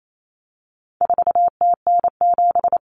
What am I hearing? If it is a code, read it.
Morse code, heard as a keyed tone, about 28 wpm, 700 Hz. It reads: 4TN7
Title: Common Cut Numbers - 4-Digit 28wpm